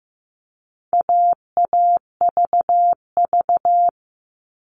Morse code, heard as a keyed tone, about 15 wpm, 700 Hz. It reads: AAVV
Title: All Cut Numbers - 4-Digit 15wpm